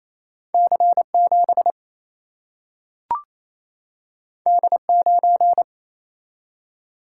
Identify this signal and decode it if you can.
Morse code, heard as a keyed tone, about 28 wpm, 700 Hz. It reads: C 7 D 9
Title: Single-Letter Single-Number 28wpm